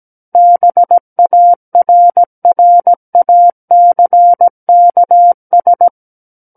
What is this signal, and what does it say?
Morse code, heard as a keyed tone, about 17 wpm, 700 Hz. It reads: BARRACKS
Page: SKCC Learning Center - Morse Code Practice Files